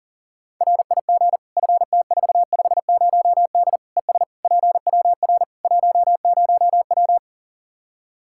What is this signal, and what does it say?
Morse code, heard as a keyed tone, about 37 wpm, 700 Hz. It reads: RIG FT450D ES PWR 10W
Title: Practice QSO - Exchange 2 - Info - 40wpm